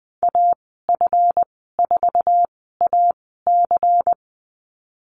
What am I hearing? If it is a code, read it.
Morse code, heard as a keyed tone, about 20 wpm, spaced as for 15 wpm, 700 Hz. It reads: AF4AC
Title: 173 Lesson 28g - Number 9 - Call Signs 20wpm